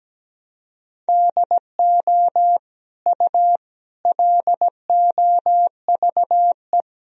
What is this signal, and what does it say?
Morse code, heard as a keyed tone, about 17 wpm, 700 Hz. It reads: DO U LOVE
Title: Sentences from Top 400 Words and Abbreviations 17wpm